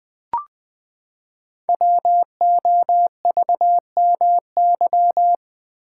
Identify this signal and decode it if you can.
Morse code, heard as a keyed tone, about 20 wpm, 700 Hz. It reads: WOVMY
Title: Code Groups - 5 Letters - 20wpm